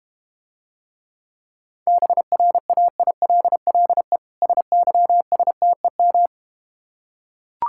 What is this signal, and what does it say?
Morse code, heard as a keyed tone, about 32 wpm, 700 Hz. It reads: BRAILLE SYSTEM